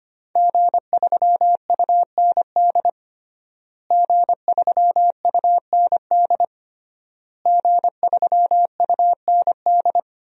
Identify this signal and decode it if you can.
Morse code, heard as a keyed tone, about 25 wpm, 700 Hz. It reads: G3UND G3UND G3UND
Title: DXCC-100 Call Signs - 5 Characters - Repeated 3x - 25wpm